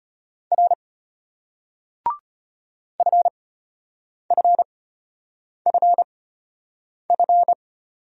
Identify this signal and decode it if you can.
Morse code, heard as a keyed tone, about 33 wpm, spaced as for 8 wpm, 690 Hz. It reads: R FFFF